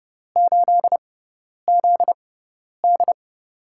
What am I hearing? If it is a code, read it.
Morse code, heard as a keyed tone, about 30 wpm, spaced as for 10 wpm, 700 Hz. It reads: 8ZD
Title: USA Call Signs - 4 Characters - 6x Character Spacing - 30wpm